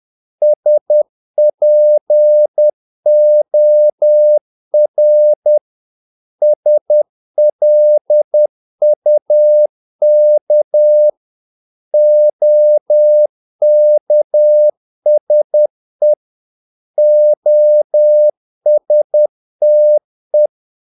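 Morse code, heard as a keyed tone, about 10 wpm, 600 Hz. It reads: SPOR SLUK OKSE OSTE
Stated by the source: Morse practice of short Danish words at 10 wpm, from CW med Gnister.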